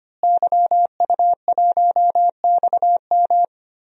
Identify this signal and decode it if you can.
Morse code, heard as a keyed tone, about 25 wpm, 700 Hz. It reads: YU1XM
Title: DXCC-100 Call Signs - 5 Characters - Repeated 3x - 25wpm